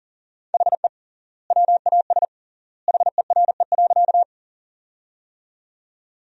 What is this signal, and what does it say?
Morse code, heard as a keyed tone, about 40 wpm, 700 Hz. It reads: HE WAS HERE.
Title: 3x Word-Spacing - Sentences from Top 200 Words - 40wpm